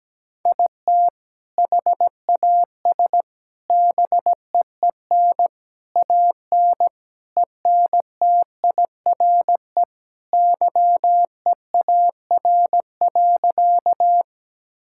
Morse code, heard as a keyed tone, about 17 wpm, 700 Hz. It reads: IT HAS BEEN AN ENTIRE YEAR.
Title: Sentences from Top 1000 Words - Review Part 5 - 17wpm